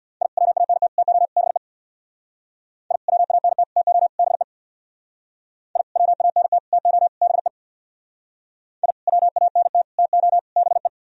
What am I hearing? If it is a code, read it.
Morse code, heard as a keyed tone, about 64 wpm, 700 Hz. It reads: I WANT TO BE I WANT TO BE I WANT TO BE I WANT TO BE